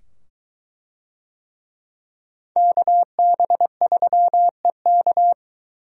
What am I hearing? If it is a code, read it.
Morse code, heard as a keyed tone, about 22 wpm, 700 Hz. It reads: KB3EK